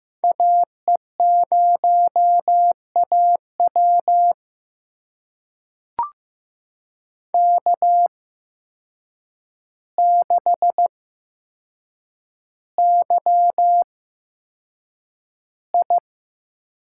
Morse code, heard as a keyed tone, about 15 wpm, spaced as for 7 wpm, 700 Hz. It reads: AE0AW K6YI
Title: USA Call Signs - 8x Character Spacing - 15wpm